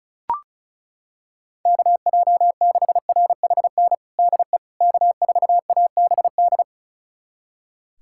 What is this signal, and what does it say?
Morse code, heard as a keyed tone, about 34 wpm, 690 Hz. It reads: KJ6RHN DE K4ABD
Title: USA Callsign DE Callsign 35wpm